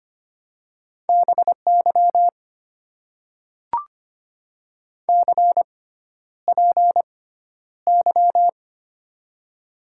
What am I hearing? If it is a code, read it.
Morse code, heard as a keyed tone, about 25 wpm, spaced as for 11 wpm, 700 Hz. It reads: BY CPY
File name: Top 90 QSO Elements - 6x Character Spacing - 25wpm